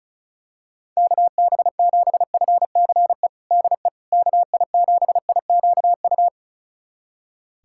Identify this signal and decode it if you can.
Morse code, heard as a keyed tone, about 36 wpm, 700 Hz. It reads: KB7FCE DE KI7IQU
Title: USA Callsign DE Callsign - 6 Characters - 35wpm